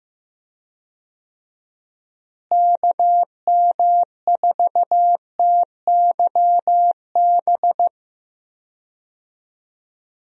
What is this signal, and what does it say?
Morse code, heard as a keyed tone, about 15 wpm, 700 Hz. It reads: KM4TYB
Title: USA Call Signs - 6 Characters - Repeated 2x - 15wpm